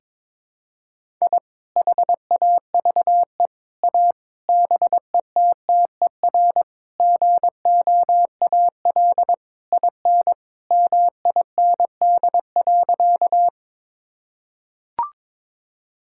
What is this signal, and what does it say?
Morse code, heard as a keyed tone, about 22 wpm, 700 Hz. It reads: I HAVE A BETTER GOAL IN MIND.
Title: Sentences from Top 1000 Words - Review Part 5 - 22wpm